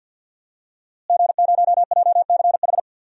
Morse code, heard as a keyed tone, about 49 wpm, 700 Hz. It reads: G0JCH
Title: DXCC-100 Call Signs - 5 Characters - Repeated 2x - 50wpm